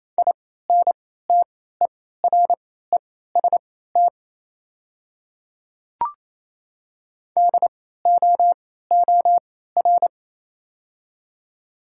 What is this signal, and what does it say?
Morse code, heard as a keyed tone, about 28 wpm, spaced as for 16 wpm, 700 Hz. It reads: INTEREST DOOR
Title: 3x Character Spacing - Top 500 Words - 28wpm